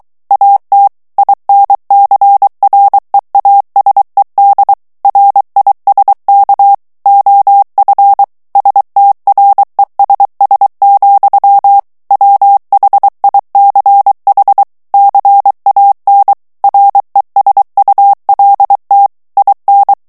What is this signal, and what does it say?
Morse code, heard as a keyed tone, about 23 wpm, 800 Hz. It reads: AT INCREASED RISK OF STRESS, WHICH CAN RESULT IN
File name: morse mp3 / ANGLAIS texte clair 1 vitesse 22 mots minute